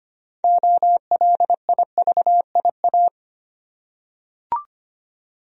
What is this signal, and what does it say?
Morse code, heard as a keyed tone, about 25 wpm, 700 Hz. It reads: OLIVIA
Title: USA Common Names - 25wpm